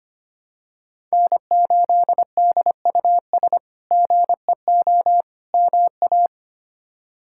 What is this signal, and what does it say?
Morse code, heard as a keyed tone, about 25 wpm, 700 Hz. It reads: N8DUS GEO MA
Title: K1USN SST QSO Practice - 25wpm